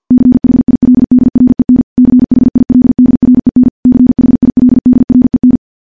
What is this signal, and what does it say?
Morse code, heard as a keyed tone, about 50 wpm, 250 Hz. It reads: CHILDREN CHILDREN CHILDREN
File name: morse / morse-training-50WPM